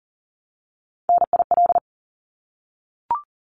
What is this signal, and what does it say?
Morse code, heard as a keyed tone, about 40 wpm, 700 Hz. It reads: NIL
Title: Abbreviations Repeated 3x - 40wpm